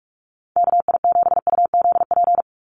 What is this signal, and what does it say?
Morse code, heard as a keyed tone, about 44 wpm, 700 Hz. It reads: KI7UZP
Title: USA Call Signs 45wpm